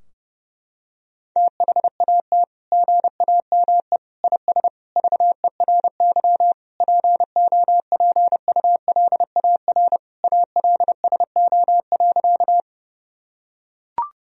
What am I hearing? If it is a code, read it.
Morse code, heard as a keyed tone, about 32 wpm, 700 Hz. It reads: THAT GAME IS VERY POPULAR ALSO.
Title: Sentences from Top 1000 Words - Review Part 3 - 30wpm